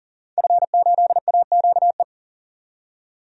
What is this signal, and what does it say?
Morse code, heard as a keyed tone, about 40 wpm, 700 Hz. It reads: F8AQE
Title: DXCC-100 Call Signs - 5 Characters - 40wpm